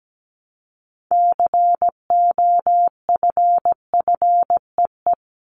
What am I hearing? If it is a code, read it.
Morse code, heard as a keyed tone, about 17 wpm, 700 Hz. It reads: COFFEE
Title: Top 1751-2000 Words 17wpm